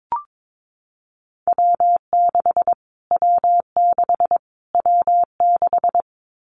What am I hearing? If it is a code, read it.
Morse code, heard as a keyed tone, about 22 wpm, 700 Hz. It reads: W6 W6 W6
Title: Callsign Prefixes for Top 25 DXCC - Repeated 3x - 22wpm